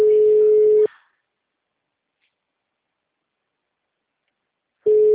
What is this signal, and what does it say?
Ringback tone (Europe, 425 Hz), with speech.
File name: out- -2044- - - .23011